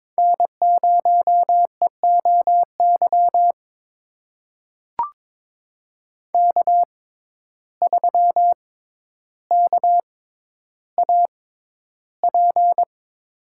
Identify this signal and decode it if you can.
Morse code, heard as a keyed tone, about 22 wpm, spaced as for 11 wpm, 700 Hz. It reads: N0EOY K3KAP
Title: USA Call Signs - 5 Characters - 6x Character Spacing - 22wpm